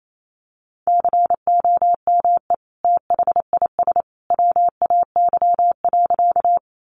Morse code, heard as a keyed tone, about 28 wpm, 700 Hz. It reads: COME THIS WAY.
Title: Sentences from Top 100 Words 28wpm